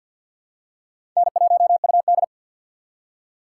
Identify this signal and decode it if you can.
Morse code, heard as a keyed tone, about 52 wpm, 700 Hz. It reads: N1UD
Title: USA Call Signs - 4 Characters - 6x Character Spacing - 50wpm